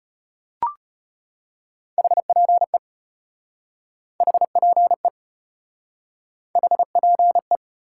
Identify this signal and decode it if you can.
Morse code, heard as a keyed tone, about 32 wpm, 700 Hz. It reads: HPE HPE HPE